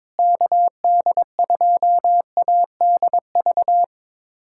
Morse code, heard as a keyed tone, about 21 wpm, 700 Hz. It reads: KD2ADV
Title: USA Callsign DE Callsign - 6 Characters - 22wpm